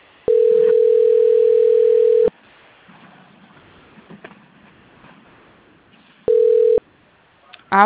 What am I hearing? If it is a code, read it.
Ringback tone (North America), 440 + 480 Hz.
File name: rg-1600-unknown- - - .86867